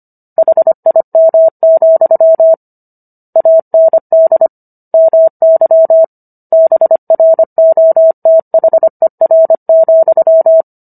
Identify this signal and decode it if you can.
Morse code, heard as a keyed tone, about 25 wpm, 650 Hz. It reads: HIM, AND MY BROTHER,